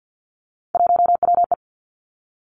Morse code, heard as a keyed tone, about 50 wpm, 700 Hz. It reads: 1WE
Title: Three Challenging Letter-Number Combinations 50wpm